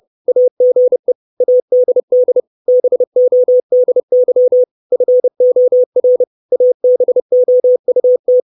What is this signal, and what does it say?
Morse code, heard as a keyed tone, about 30 wpm, 500 Hz. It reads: AGE ADD BODY FOR ABOUT
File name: 632 Words 30WPM